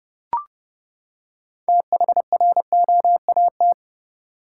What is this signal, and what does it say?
Morse code, heard as a keyed tone, about 30 wpm, 700 Hz. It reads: THROAT
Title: Six Letter Words - 30wpm